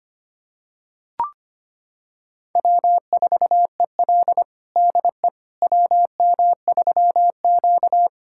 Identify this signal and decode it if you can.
Morse code, heard as a keyed tone, about 25 wpm, 700 Hz. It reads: W4EL DE WM3Q
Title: USA Callsign DE Callsign - 4 Characters - 25wpm